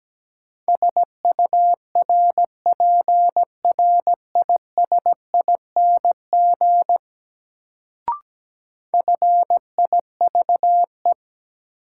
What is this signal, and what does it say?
Morse code, heard as a keyed tone, about 17 wpm, 700 Hz. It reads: SURPRISING FIVE
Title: Sets of 5 Words - New Word Drill - 17wpm